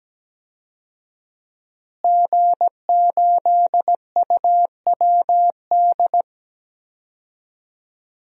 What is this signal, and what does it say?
Morse code, heard as a keyed tone, about 17 wpm, 700 Hz. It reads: G8UWD
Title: DXCC-100 Call Signs - 5 Characters - Repeated 2x - 17wpm